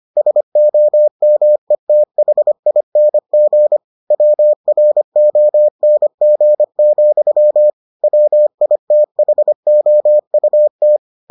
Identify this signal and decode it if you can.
Morse code, heard as a keyed tone, about 25 wpm, 600 Hz. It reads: SOMETHING WRONG, WITHOUT